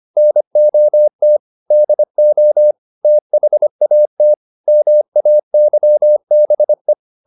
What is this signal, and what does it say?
Morse code, heard as a keyed tone, about 25 wpm, 600 Hz. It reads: NOT DO THAT MAYBE
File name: the old man and the sea part 1 25wpm